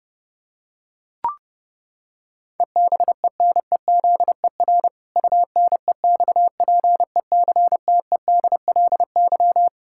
Morse code, heard as a keyed tone, about 30 wpm, 700 Hz. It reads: EBENEZER UNEXPECTEDLY
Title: Pangram Challenge v2 - 30wpm